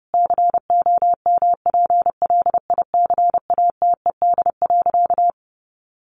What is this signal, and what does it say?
Morse code, heard as a keyed tone, about 30 wpm, 700 Hz. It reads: COMPLICATED.
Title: Sentences from Top 1200 Words 30wpm